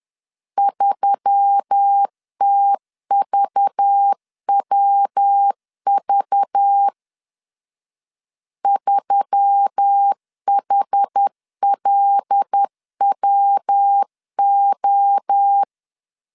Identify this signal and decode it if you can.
Morse code, heard as a keyed tone, about 10 wpm, 800 Hz. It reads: 3TVWV 3HLWO
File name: LEZIONI / 6 PKX0 / 6 PKX 10wpm 2426